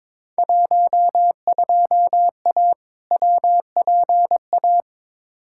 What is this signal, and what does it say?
Morse code, heard as a keyed tone, about 22 wpm, 700 Hz. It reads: 12A WPA
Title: Field Day Exchanges Encore - 22wpm